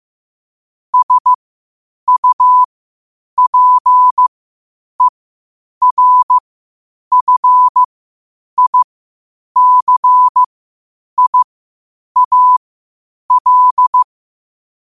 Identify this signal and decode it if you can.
Morse code, heard as a keyed tone, about 15 wpm, spaced as for 8 wpm, 1000 Hz. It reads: SUPERFICIAL
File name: qotdfiles / 8wpm